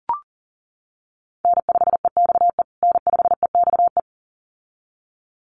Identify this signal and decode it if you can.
Morse code, heard as a keyed tone, about 40 wpm, 700 Hz. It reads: N5EXE N5EXE
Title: USA Call Signs Repeated 2x 40wpm